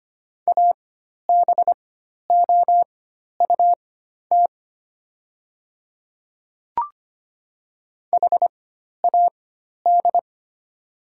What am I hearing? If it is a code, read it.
Morse code, heard as a keyed tone, about 25 wpm, spaced as for 12 wpm, 700 Hz. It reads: ABOUT HAD
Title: 4x Character Spacing - Top 300 Words - 25wpm